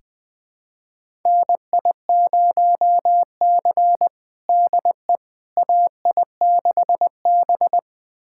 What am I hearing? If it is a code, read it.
Morse code, heard as a keyed tone, about 20 wpm, 700 Hz. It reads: NI0C DE AI6B
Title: USA Callsign DE Callsign - 4 Characters - 20wpm